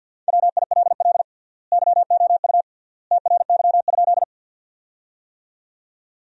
Morse code, heard as a keyed tone, about 50 wpm, 700 Hz. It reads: WILL YOU TRY?
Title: 3x Word-Spacing - Sentences from Top 200 Words - 50wpm